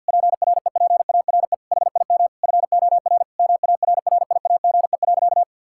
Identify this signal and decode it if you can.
Morse code, heard as a keyed tone, about 50 wpm, 700 Hz. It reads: PREPARE HIM FOR MARRIAGE.
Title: Sentences from Top 1000 Words - Review Part 6 - 50wpm